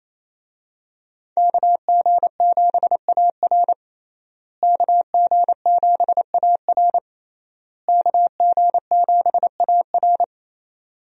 Morse code, heard as a keyed tone, about 28 wpm, 700 Hz. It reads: KG7AR KG7AR KG7AR
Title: USA Call Signs - 5 Characters - Repeated 3x - 28wpm